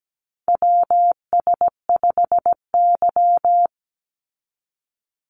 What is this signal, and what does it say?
Morse code, heard as a keyed tone, about 17 wpm, 700 Hz. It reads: WS5Y
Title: USA Call Signs 17wpm